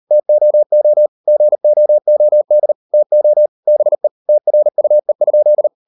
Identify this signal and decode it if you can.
Morse code, heard as a keyed tone, about 40 wpm, 600 Hz. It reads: TOO GOOD TO BE TRUE?